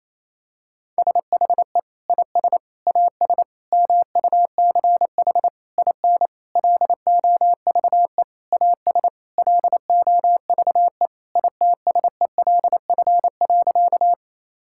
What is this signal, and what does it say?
Morse code, heard as a keyed tone, about 28 wpm, 700 Hz. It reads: SHE IS AS MUCH IN LOVE AS LOVE ITSELF.
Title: Sentences from Top 1000 Words - Review Part 5 - 28wpm